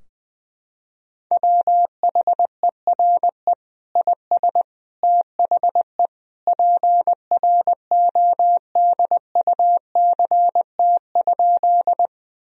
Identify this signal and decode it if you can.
Morse code, heard as a keyed tone, about 20 wpm, 700 Hz. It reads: WHERE IS THE PRODUCT?
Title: Sentences from Top 1000 Words - Review Part 2 - 20wpm